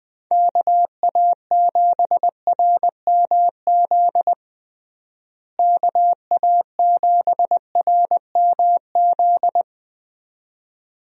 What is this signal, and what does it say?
Morse code, heard as a keyed tone, about 20 wpm, 700 Hz. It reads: KA7RMZ KA7RMZ
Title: USA Call Signs - 6 Characters - Repeated 2x - 20wpm